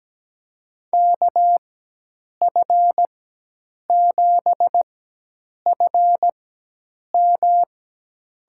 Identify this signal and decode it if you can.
Morse code, heard as a keyed tone, about 17 wpm, spaced as for 8 wpm, 700 Hz. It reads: KF7FM
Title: USA Call Signs - 6 Characters - 4x Character Spacing - 17wpm